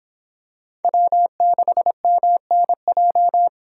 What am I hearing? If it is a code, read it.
Morse code, heard as a keyed tone, about 25 wpm, 700 Hz. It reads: W6MNJ